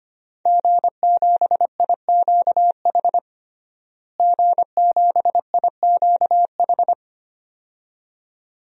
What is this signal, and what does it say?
Morse code, heard as a keyed tone, about 25 wpm, 700 Hz. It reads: G7IQH G7IQH
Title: DXCC-100 Call Signs - 5 Characters - Repeated 2x - 25wpm